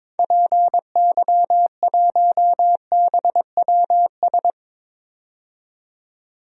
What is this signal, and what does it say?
Morse code, heard as a keyed tone, about 22 wpm, 700 Hz. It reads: PY1BWS
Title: DXCC-100 Call Signs - 6 Characters - 22wpm